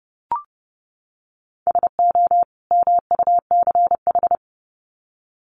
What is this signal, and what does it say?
Morse code, heard as a keyed tone, about 30 wpm, 700 Hz. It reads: SO MUCH
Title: Sets of 2 Words 30wpm